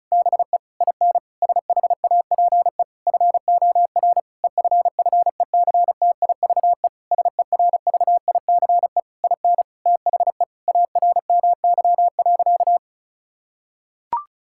Morse code, heard as a keyed tone, about 34 wpm, 700 Hz. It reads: BE IN SHAPE FOR EFFECTIVE SERVICE IN THE ARMY.
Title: Sentences from Top 1000 Words - Review Part 6 - 35wpm